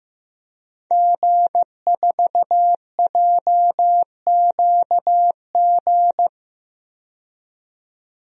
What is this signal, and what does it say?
Morse code, heard as a keyed tone, about 15 wpm, 700 Hz. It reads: G4JQG
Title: DXCC-100 Call Signs - 5 Characters - 15wpm